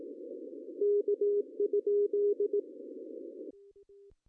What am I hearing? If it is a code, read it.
Morse code, heard as a keyed tone, about 18 wpm, 410 Hz. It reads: K?
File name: CW / cqww-cw-k5tr